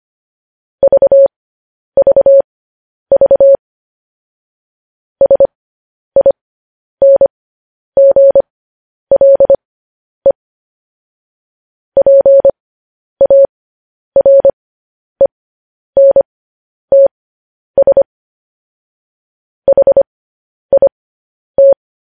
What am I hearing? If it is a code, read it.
Morse code, heard as a keyed tone, about 25 wpm, spaced as for 10 wpm, 550 Hz. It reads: VVV SINGLE PARENTS HIT
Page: news-headlines-in-morse-code-at-10-wpm